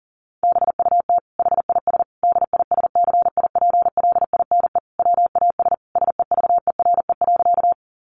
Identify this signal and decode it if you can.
Morse code, heard as a keyed tone, about 40 wpm, 700 Hz. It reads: BUT HIS DISCIPLINE WAS SEVERE.
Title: Sentences from Top 1400 Words 40wpm